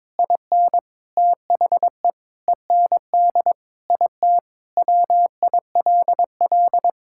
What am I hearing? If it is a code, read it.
Morse code, heard as a keyed tone, about 21 wpm, 700 Hz. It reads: IN THE END IT WILL
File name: practice / 22wpm / Sentences with 3-Word Phrases - 22wpm